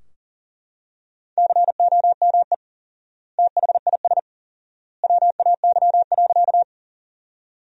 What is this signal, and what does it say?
Morse code, heard as a keyed tone, about 40 wpm, 700 Hz. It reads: COME THIS WAY.
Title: 4x Word-Spacing - Sentences from Top 100 Words - 40wpm